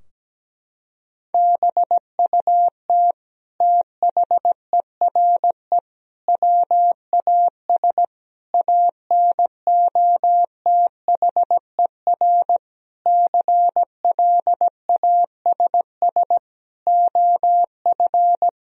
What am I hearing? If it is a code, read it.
Morse code, heard as a keyed tone, about 17 wpm, 700 Hz. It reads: BUT THERE WAS ANOTHER CLASS OF
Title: Sentences from Top 1000 Words - Review Part 2 - 17wpm